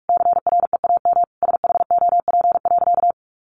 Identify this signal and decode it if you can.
Morse code, heard as a keyed tone, about 44 wpm, 700 Hz. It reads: CREAM SHOP.
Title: Additional Easy Phrases 45wpm